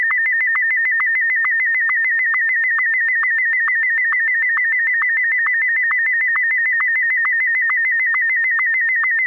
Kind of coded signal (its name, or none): SSTV (slow-scan television)